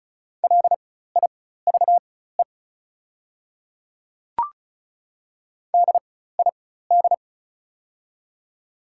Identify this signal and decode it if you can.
Morse code, heard as a keyed tone, about 36 wpm, spaced as for 16 wpm, 700 Hz. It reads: LIVE DID
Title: 4x Character Spacing - Top 500 Words - 35wpm